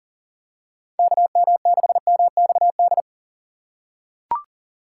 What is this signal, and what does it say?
Morse code, heard as a keyed tone, about 40 wpm, 700 Hz. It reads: KM6MXD
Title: USA Call Signs - 6 Characters - 4x Character Spacing - 40wpm